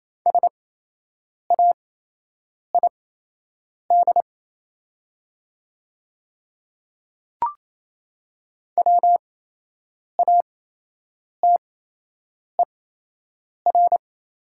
Morse code, heard as a keyed tone, about 27 wpm, spaced as for 8 wpm, 700 Hz. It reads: SAID WATER
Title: 8x Character Spacing - Top 100 Words - 28wpm